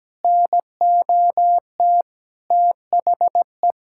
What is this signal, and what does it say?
Morse code, heard as a keyed tone, about 17 wpm, 700 Hz. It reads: NOT THE
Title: Sentences from Top 1600 Words 17wpm